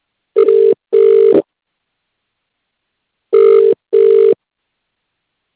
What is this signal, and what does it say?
Ringback tone (UK), 400 + 450 Hz.